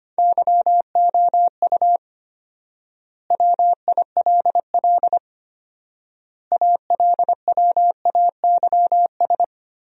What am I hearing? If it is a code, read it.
Morse code, heard as a keyed tone, about 25 wpm, 700 Hz. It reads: YOU WILL ALWAYS
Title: 4x Word-Spacing - Sentences from Top 400 Words - 25wpm